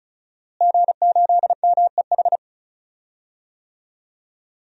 Morse code, heard as a keyed tone, about 36 wpm, 700 Hz. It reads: G8MEH
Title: DXCC-100 Call Signs - 5 Characters - Repeated 3x - 35wpm